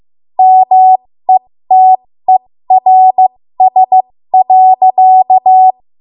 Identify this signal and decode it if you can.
Morse code, heard as a keyed tone, about 15 wpm, spaced as for 13 wpm, 750 Hz. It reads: METERS.
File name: Morse / 13WPM